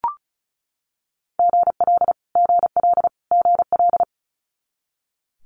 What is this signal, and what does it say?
Morse code, heard as a keyed tone, about 36 wpm, 700 Hz. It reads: GL GL GL
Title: Multi-letter Phonemes Repeated 3x 35wpm